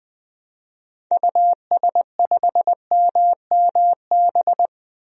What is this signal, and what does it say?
Morse code, heard as a keyed tone, about 21 wpm, 700 Hz. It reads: US5MMB
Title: DXCC-100 Call Signs - 6 Characters - 20wpm